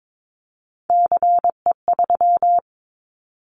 Morse code, heard as a keyed tone, about 22 wpm, 700 Hz. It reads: CE3
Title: Callsign Prefixes for Top 100 DXCC - 22wpm